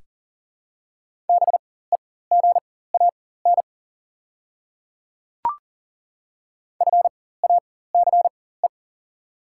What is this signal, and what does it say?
Morse code, heard as a keyed tone, about 40 wpm, spaced as for 19 wpm, 690 Hz. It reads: BEGAN FACE
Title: 4x Character Spacing - Top 300 Words - 40wpm